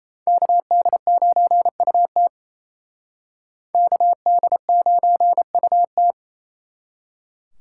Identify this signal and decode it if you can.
Morse code, heard as a keyed tone, about 35 wpm, 700 Hz. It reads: KD9UT KD9UT
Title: Speed Racing - USA 2x2 Call Signs 28wpm